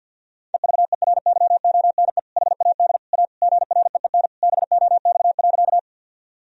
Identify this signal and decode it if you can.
Morse code, heard as a keyed tone, about 52 wpm, 700 Hz. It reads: EVERYONE HAD A GREEN BOX.
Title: Sentences from Top 1000 Words - Review Part 2 - 50wpm